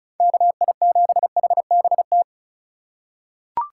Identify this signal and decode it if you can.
Morse code, heard as a keyed tone, about 36 wpm, 700 Hz. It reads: KI7HBT
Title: USA Call Signs - 2x Character Spacing - 35wpm